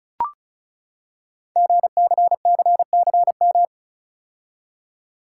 Morse code, heard as a keyed tone, about 36 wpm, 690 Hz. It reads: GCCCM
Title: Code Groups - 5 Letters - 35wpm